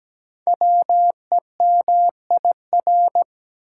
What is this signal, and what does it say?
Morse code, heard as a keyed tone, about 17 wpm, 700 Hz. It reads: WEMIR
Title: Code Groups - 5 Letters - 17wpm